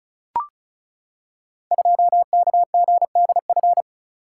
Morse code, heard as a keyed tone, about 36 wpm, 700 Hz. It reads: 2KGDF
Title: Code Groups - 5 Letter:Numbers - 35wpm